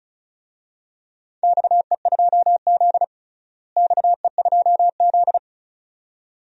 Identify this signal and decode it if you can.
Morse code, heard as a keyed tone, about 34 wpm, 700 Hz. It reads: XE2Z XE2Z
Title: DXCC-100 Call Signs - 4 Characters - Repeated 2x - 35wpm